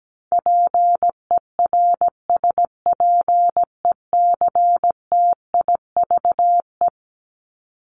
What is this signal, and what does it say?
Morse code, heard as a keyed tone, about 17 wpm, 700 Hz. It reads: PERSPECTIVE
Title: Top 1001-1250 Words 17wpm